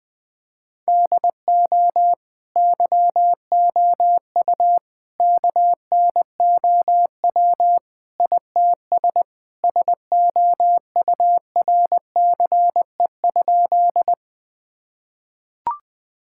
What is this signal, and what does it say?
Morse code, heard as a keyed tone, about 20 wpm, 700 Hz. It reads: DO YOU KNOW ITS SOURCE?
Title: Sentences from Top 1000 Words - Review Part 5 - 20wpm